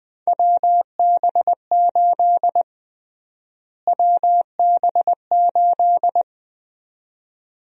Morse code, heard as a keyed tone, about 20 wpm, 700 Hz. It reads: WB8 WB8
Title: USA Call Sign Prefixes - Repeated 2x - 20wpm